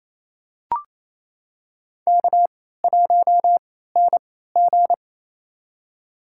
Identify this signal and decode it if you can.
Morse code, heard as a keyed tone, about 28 wpm, spaced as for 16 wpm, 700 Hz. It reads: K1NG
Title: USA Call Signs - 4 Characters - 3x Character Spacing - 28wpm